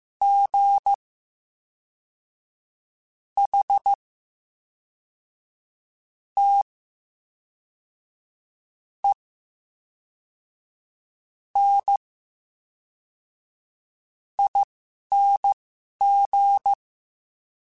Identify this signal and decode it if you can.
Morse code, heard as a keyed tone, about 15 wpm, spaced as for 11 wpm, 800 Hz. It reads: G H T E N ING
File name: Special INGs / Special INGs-mp3s / VerySlowToFaster Ing @10